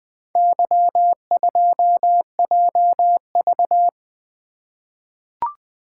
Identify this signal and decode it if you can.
Morse code, heard as a keyed tone, about 20 wpm, 700 Hz. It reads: Y2JV
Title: Four Letter-Number Combinations - 20wpm